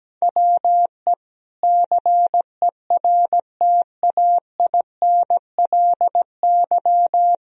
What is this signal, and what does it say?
Morse code, heard as a keyed tone, about 17 wpm, 700 Hz. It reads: WE CERTAINLY
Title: Sentences from Top 1000 Words - Review Part 4 - 17wpm